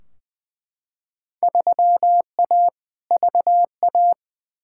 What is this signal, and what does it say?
Morse code, heard as a keyed tone, about 21 wpm, 700 Hz. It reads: 3A VA
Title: Field Day Exchanges - 20wpm